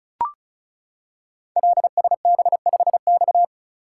Morse code, heard as a keyed tone, about 36 wpm, 690 Hz. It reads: LSB5X
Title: Code Groups - 5 Letter:Numbers - 35wpm